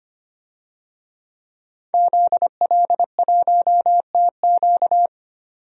Morse code, heard as a keyed tone, about 25 wpm, 700 Hz. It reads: ZL1TQ
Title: DXCC-100 Call Signs - 5 Characters - 25wpm